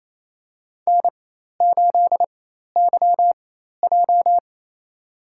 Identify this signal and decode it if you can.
Morse code, heard as a keyed tone, about 28 wpm, spaced as for 13 wpm, 700 Hz. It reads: N8YJ
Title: USA Call Signs - 4 Characters - 4x Character Spacing - 28wpm